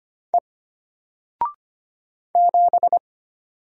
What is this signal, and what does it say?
Morse code, heard as a keyed tone, about 25 wpm, 700 Hz. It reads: E 7
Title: Single Letter-Number Alt 25wpm